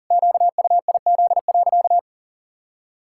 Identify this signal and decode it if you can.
Morse code, heard as a keyed tone, about 40 wpm, 700 Hz. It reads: QUIZ.
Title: Pangram Challenge v2 - 40wpm